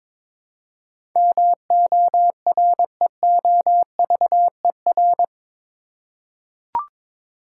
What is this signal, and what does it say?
Morse code, heard as a keyed tone, about 22 wpm, 700 Hz. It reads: MOREOVER
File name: practice / 22wpm / Eight Letter Words - 22wpm